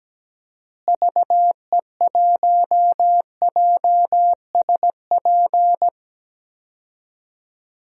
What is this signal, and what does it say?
Morse code, heard as a keyed tone, about 17 wpm, 700 Hz. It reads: VE1JSP
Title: DXCC-100 Call Signs - 6 Characters - Repeated 2x - 17wpm